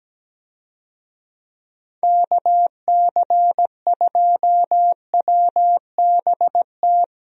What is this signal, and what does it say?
Morse code, heard as a keyed tone, about 17 wpm, 700 Hz. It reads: KC2WBT
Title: USA Call Signs - 6 Characters - Repeated 3x - 17wpm